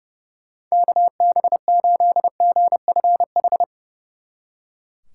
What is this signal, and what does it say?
Morse code, heard as a keyed tone, about 37 wpm, 700 Hz. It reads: KB8GFH
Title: Speed Racing - USA 2x3 Call Signs 30wpm